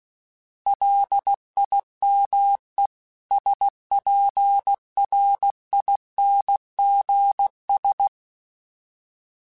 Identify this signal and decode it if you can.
Morse code, heard as a keyed tone, about 16 wpm, 800 Hz. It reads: LIME SPRINGS
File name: fromSMs / FromSMs mp3s / Cities@15